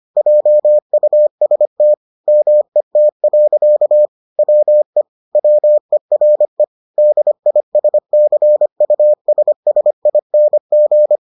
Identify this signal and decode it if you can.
Morse code, heard as a keyed tone, about 25 wpm, 600 Hz. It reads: JUST MET. WE WERE DISCUSSING